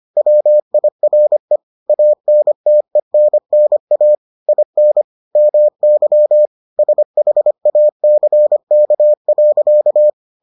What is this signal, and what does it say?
Morse code, heard as a keyed tone, about 25 wpm, 600 Hz. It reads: WIRE ANTENNA IN MY SHACK.